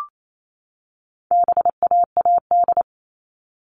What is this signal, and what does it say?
Morse code, heard as a keyed tone, about 28 wpm, 700 Hz. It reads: BAAD
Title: All Cut Numbers - 4-Digit 28wpm